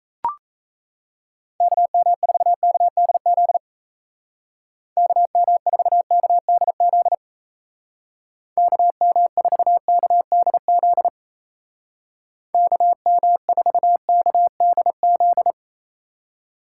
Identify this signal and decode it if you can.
Morse code, heard as a keyed tone, about 34 wpm, 700 Hz. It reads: KM4KDZ KM4KDZ KM4KDZ KM4KDZ
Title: Speed Racing - USA 2x3 Call Signs 28wpm